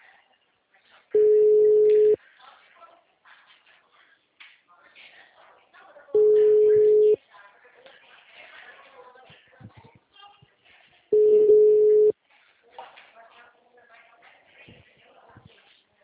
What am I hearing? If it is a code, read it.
Ringback tone (Europe, 425 Hz), with speech.
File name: out- -2012- - - .66446